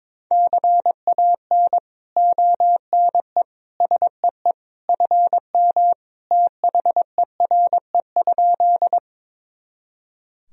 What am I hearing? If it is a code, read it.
Morse code, heard as a keyed tone, about 22 wpm, 700 Hz. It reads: CAN ONE SEE FM THERE?
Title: Sentences from Top 100 Words and Abbreviations 22wpm